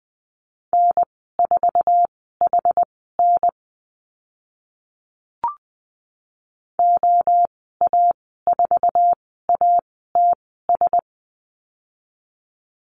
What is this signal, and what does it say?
Morse code, heard as a keyed tone, about 20 wpm, spaced as for 15 wpm, 700 Hz. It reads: N4HN OA4ATS
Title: 062 Lesson 11g - Letter h - Call Signs 20wpm